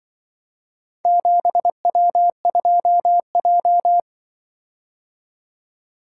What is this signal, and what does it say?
Morse code, heard as a keyed tone, about 24 wpm, 700 Hz. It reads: 7W2J
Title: Speed-Racing - Four Letter-Number Combinations - 20wpm